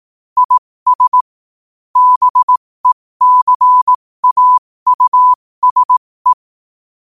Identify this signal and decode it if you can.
Morse code, heard as a keyed tone, about 18 wpm, spaced as for 15 wpm, 1000 Hz. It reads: IS BECAUSE
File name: qotdfiles / 15wpm